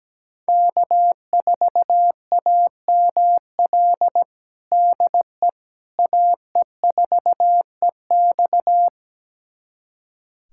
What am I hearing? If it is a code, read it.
Morse code, heard as a keyed tone, about 17 wpm, 700 Hz. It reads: K4AML DE AE4EX
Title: USA Callsign DE Callsign - 5 Characters - 17wpm